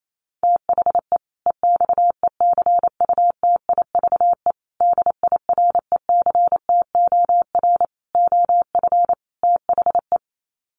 Morse code, heard as a keyed tone, about 28 wpm, 700 Hz. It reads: THE EXECUTIVE DIRECTOR OF THE
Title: Sets of 5 Words Encore 28wpm